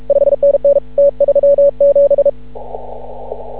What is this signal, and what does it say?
Morse code, heard as a keyed tone, about 36 wpm, 560 Hz. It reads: 5NN T37
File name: CW / ha6oi 2